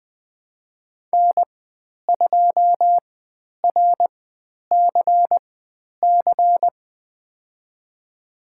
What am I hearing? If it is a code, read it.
Morse code, heard as a keyed tone, about 20 wpm, spaced as for 10 wpm, 700 Hz. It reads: N2RCC
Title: 102 Lesson 17g - Letter u - Call Signs 20wpm